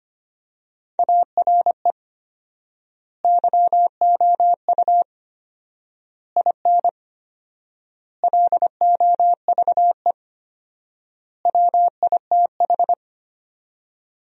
Morse code, heard as a keyed tone, about 25 wpm, 700 Hz. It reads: ARE YOU IN LOVE WITH
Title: 4x Word-Spacing - Sentences from Top 400 Words - 25wpm